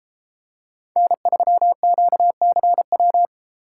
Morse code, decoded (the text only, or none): N3QCW